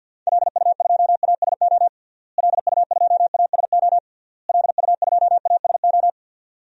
Morse code, heard as a keyed tone, about 52 wpm, 700 Hz. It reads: LU2ASO LU2ASO LU2ASO
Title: DXCC-100 Call Signs - 6 Characters - Repeated 3x - 50wpm